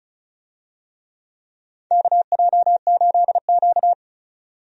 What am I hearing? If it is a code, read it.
Morse code, heard as a keyed tone, about 36 wpm, 700 Hz. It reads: KJ8Q
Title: USA Call Signs - 4 Characters - 35wpm